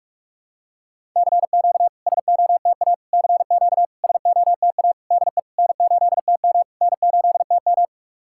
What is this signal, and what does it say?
Morse code, heard as a keyed tone, about 48 wpm, 700 Hz. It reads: CQ SOTA CQ SOTA DE N8TM N8TM
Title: SOTA QSO Practice 45wpm